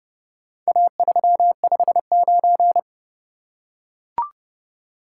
Morse code, heard as a keyed tone, about 30 wpm, 700 Hz. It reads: A359